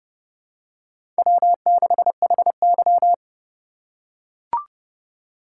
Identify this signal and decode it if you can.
Morse code, heard as a keyed tone, about 31 wpm, 700 Hz. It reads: W6HY
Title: USA Call Signs - 4 Characters - 2x Character Spacing - 30wpm